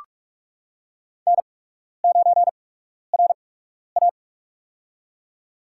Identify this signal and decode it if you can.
Morse code, heard as a keyed tone, about 45 wpm, spaced as for 12 wpm, 700 Hz. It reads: N9RA
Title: USA Call Signs - 4 Characters - 8x Character Spacing - 45wpm